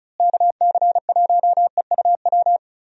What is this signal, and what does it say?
Morse code, heard as a keyed tone, about 34 wpm, 700 Hz. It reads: KC1EUW
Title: USA Call Signs - 6 Characters - Repeated 3x - 35wpm